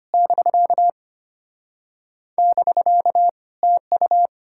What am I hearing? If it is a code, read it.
Morse code, heard as a keyed tone, about 33 wpm, 700 Hz. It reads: <BK> <BK> TU